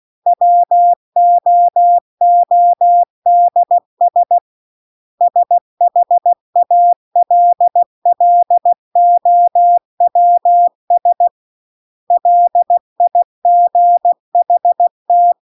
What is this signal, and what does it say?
Morse code, heard as a keyed tone, about 16 wpm, 700 Hz. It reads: WOODS SHALLOWS LIGHT